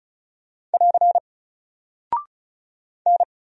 Morse code, heard as a keyed tone, about 36 wpm, 700 Hz. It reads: <AR> N
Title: Single Character (Letter:Number:Punctuation:Prosign) Alt 35wpm